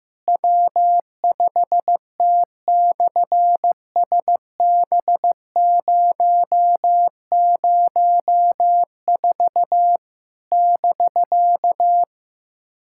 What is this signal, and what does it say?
Morse code, heard as a keyed tone, about 15 wpm, 700 Hz. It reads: W5T/SB004 <BK>
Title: SOTA QSO Practice 15wpm